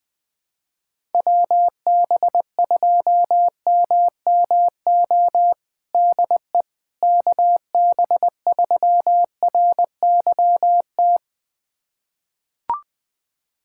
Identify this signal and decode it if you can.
Morse code, heard as a keyed tone, about 20 wpm, 700 Hz. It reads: WB2MMO DE KB3RYT
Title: USA Callsign DE Callsign - 6 Characters - 20wpm